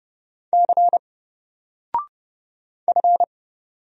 Morse code, heard as a keyed tone, about 29 wpm, 700 Hz. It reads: C F
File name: Single-Letter-Number-30wpm